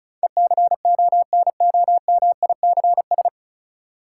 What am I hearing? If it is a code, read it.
Morse code, heard as a keyed tone, about 34 wpm, 700 Hz. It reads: ECONOMICS
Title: Sentences from Top 1000 Words - Review Part 3 - New Word Drill - 35wpm